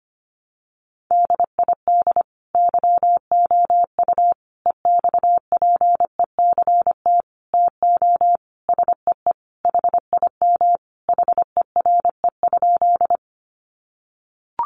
Morse code, heard as a keyed tone, about 25 wpm, 700 Hz. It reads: DID YOU EXPECT TO SEE HIM HERE?
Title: Sentences from Top 1100 Words 25wpm